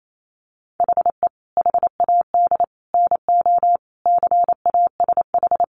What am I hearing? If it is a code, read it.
Morse code, heard as a keyed tone, about 28 wpm, 700 Hz. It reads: HE HAD NO CASH
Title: Sentences from Top 1100 Words 28wpm